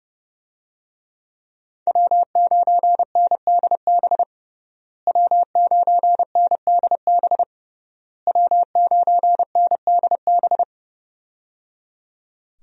Morse code, heard as a keyed tone, about 30 wpm, 700 Hz. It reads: W9NDB W9NDB W9NDB
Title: USA Call Signs - 5 Characters - Repeated 3x - 30wpm